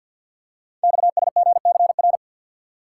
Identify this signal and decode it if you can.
Morse code, heard as a keyed tone, about 46 wpm, 700 Hz. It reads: XSGCR
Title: Code Groups - 5 Letters - 50wpm